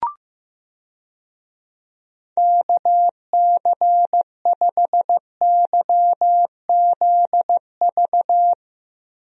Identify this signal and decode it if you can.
Morse code, heard as a keyed tone, about 15 wpm, 700 Hz. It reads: KC5YZV
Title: USA Call Signs - 6 Characters - Repeated 3x - 15wpm